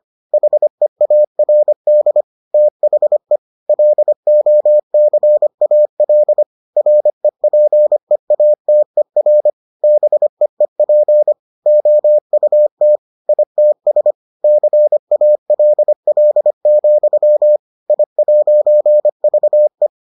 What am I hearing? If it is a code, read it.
Morse code, heard as a keyed tone, about 25 wpm, 600 Hz. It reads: HEARD THE LOCAL REPEATER BEEP OUT ITS CALL, I'VE